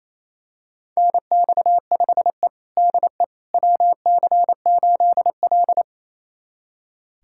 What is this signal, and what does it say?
Morse code, heard as a keyed tone, about 28 wpm, 700 Hz. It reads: NX5E DE WC8L
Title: USA Callsign DE Callsign - 4 Characters - 28wpm